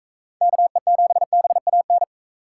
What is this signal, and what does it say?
Morse code, heard as a keyed tone, about 40 wpm, 700 Hz. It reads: KE7BAN